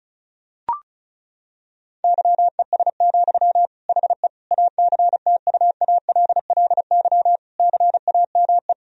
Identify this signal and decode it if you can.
Morse code, heard as a keyed tone, about 36 wpm, 700 Hz. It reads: YES, HE ACTUALLY CAME
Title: Sentences from Top 1000 Words - Review Part 4 - 35wpm